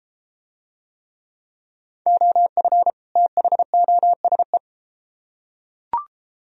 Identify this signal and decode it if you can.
Morse code, heard as a keyed tone, about 34 wpm, 700 Hz. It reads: OF THOSE